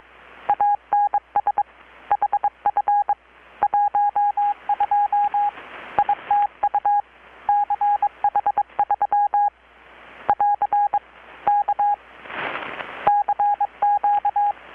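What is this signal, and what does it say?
Morse code, heard as a keyed tone, about 22 wpm, 820 Hz. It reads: ANS HF 12 ITU CH3 <AR> K CQ
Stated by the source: Morse from KPH's Night of Nights XX maritime radiotelegraphy event, received in Chiapas, Mexico about 3,800 km away on a Kenwood R-600 with a grounded 10 m long-wire antenna.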